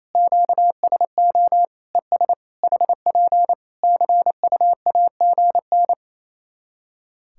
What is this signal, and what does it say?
Morse code, heard as a keyed tone, about 28 wpm, 700 Hz. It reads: QSO ES HP CUAGN
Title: K1USN SST QSO Practice 28wpm